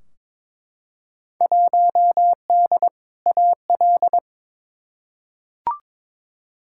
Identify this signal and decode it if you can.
Morse code, heard as a keyed tone, about 21 wpm, 700 Hz. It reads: 1D AL
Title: Field Day Exchanges - 22wpm